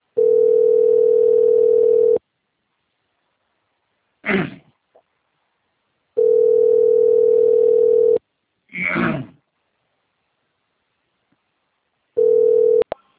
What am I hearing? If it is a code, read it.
Ringback tone (North America), 440 + 480 Hz.